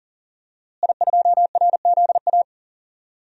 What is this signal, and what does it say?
Morse code, heard as a keyed tone, about 37 wpm, 700 Hz. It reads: I2RZA
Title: DXCC-100 Call Signs - 5 Characters - Repeated 2x - 40wpm